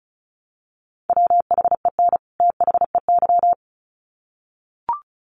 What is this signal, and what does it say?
Morse code, heard as a keyed tone, about 34 wpm, 700 Hz. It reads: WHEN THEY
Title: Sets of 2 Words 35wpm